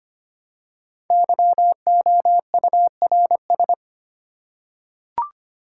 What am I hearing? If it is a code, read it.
Morse code, heard as a keyed tone, about 25 wpm, 700 Hz. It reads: YOURS
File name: 5 Letter Words - First 3 Letters Given - 25wpm